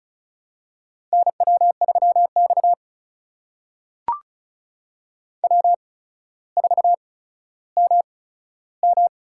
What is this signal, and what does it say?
Morse code, heard as a keyed tone, about 36 wpm, spaced as for 15 wpm, 700 Hz. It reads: NW3X W4MM
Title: USA Call Signs - 4 Characters - 8x Character Spacing - 35wpm